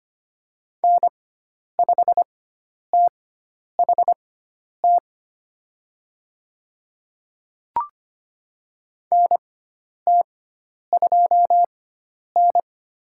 Morse code, heard as a keyed tone, about 25 wpm, spaced as for 10 wpm, 700 Hz. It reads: N5THT NT2N
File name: Lesson 16g - Letter c - Call Signs 25wpm